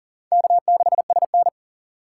Morse code, heard as a keyed tone, about 40 wpm, 700 Hz. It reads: K6SN
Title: USA Call Signs - 4 Characters - 6x Character Spacing - 40wpm